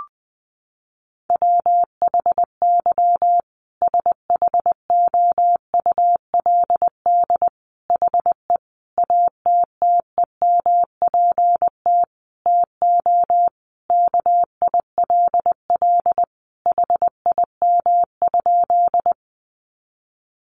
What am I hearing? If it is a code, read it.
Morse code, heard as a keyed tone, about 20 wpm, 700 Hz. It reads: WHY SHOULD HE ATTEMPT TO KILL HIM?
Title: Sentences from Top 1400 Words 20wpm